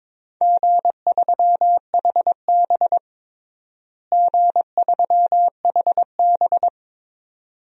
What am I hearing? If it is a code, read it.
Morse code, heard as a keyed tone, about 22 wpm, 700 Hz. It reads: G3HB G3HB
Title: DXCC-100 Call Signs - 4 Characters - Repeated 2x - 22wpm